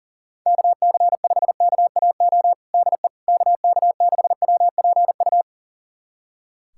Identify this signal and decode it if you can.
Morse code, heard as a keyed tone, about 40 wpm, 700 Hz. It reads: KC5KAO DE KK6WPU
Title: USA Callsign DE Callsign 40wpm